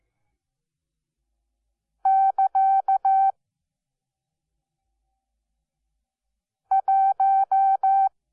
Morse code, heard as a keyed tone, about 15 wpm, 770 Hz. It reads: <KA> 1
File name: Onlinelehrgang / morsen / DJ4UF Morsen12